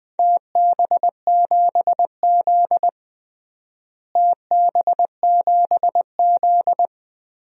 Morse code, heard as a keyed tone, about 20 wpm, 700 Hz. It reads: TB7Z TB7Z
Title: DXCC-100 Call Signs - 4 Characters - Repeated 2x - 20wpm